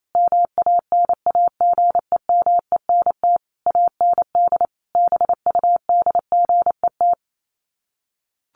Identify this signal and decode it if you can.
Morse code, heard as a keyed tone, about 28 wpm, 700 Hz. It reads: MANAGEMENT AND BUDGET
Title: Sets of 5 Words Encore2 28wpm